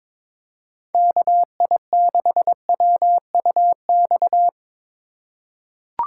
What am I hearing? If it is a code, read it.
Morse code, heard as a keyed tone, about 22 wpm, 700 Hz. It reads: KI6WUX
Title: USA Call Signs - 2x Character Spacing - 22wpm